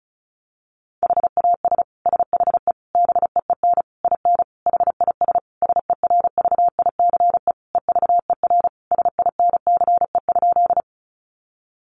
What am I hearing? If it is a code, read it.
Morse code, heard as a keyed tone, about 34 wpm, 700 Hz. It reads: HAS SHE BEEN IN HIS SERVICE EVER SINCE?
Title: Sentences from Top 700 Words 35wpm